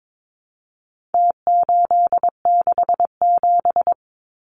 Morse code, heard as a keyed tone, about 22 wpm, 700 Hz. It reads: T867
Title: Numbers - Contest Style - 4-Digit 22wpm